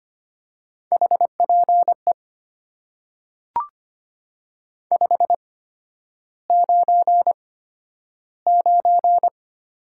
Morse code, heard as a keyed tone, about 25 wpm, spaced as for 11 wpm, 700 Hz. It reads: HPE 599
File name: Top 90 QSO Elements - 8x Character Spacing - 25wpm